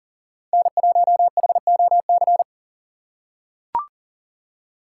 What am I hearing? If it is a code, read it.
Morse code, heard as a keyed tone, about 40 wpm, 700 Hz. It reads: N1HOC
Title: USA Call Signs - 5 Characters - 4x Character Spacing - 40wpm